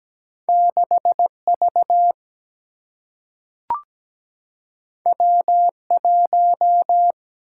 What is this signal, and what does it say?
Morse code, heard as a keyed tone, about 17 wpm, 700 Hz. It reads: 6 V W 1
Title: Single-Letter Single-Number 17wpm